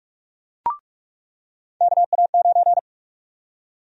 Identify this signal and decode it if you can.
Morse code, heard as a keyed tone, about 46 wpm, 700 Hz. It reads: KA9
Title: USA Call Sign Prefixes - 45wpm